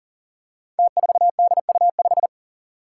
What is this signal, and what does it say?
Morse code, heard as a keyed tone, about 44 wpm, 700 Hz. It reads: T4DU5
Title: Five Challenging Letter-Number Combinations 40wpm